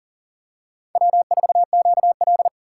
Morse code, heard as a keyed tone, about 40 wpm, 700 Hz. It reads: W4QL
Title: USA Call Signs - 6x Character Spacing - 40wpm